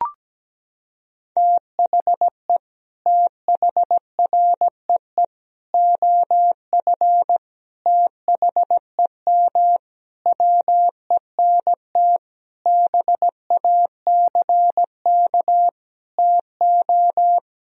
Morse code, heard as a keyed tone, about 17 wpm, 700 Hz. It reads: THE THREE OF THEM WENT BACK TO
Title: Sentences from Top 1000 Words - Review Part 1 - 17wpm